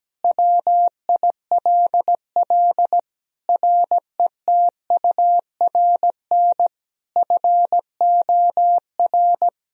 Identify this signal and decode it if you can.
Morse code, heard as a keyed tone, about 17 wpm, 700 Hz. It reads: WILL RETURN FOR
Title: Sentences from Top 1000 Words - Review Part 1 - 17wpm